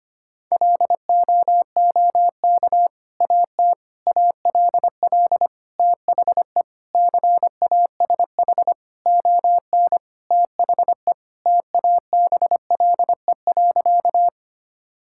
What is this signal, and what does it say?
Morse code, heard as a keyed tone, about 25 wpm, 700 Hz. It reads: LOOK AT ALL THE CASH ON THE TABLE.
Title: Sentences from Top 1000 Words - Review Part 5 - 25wpm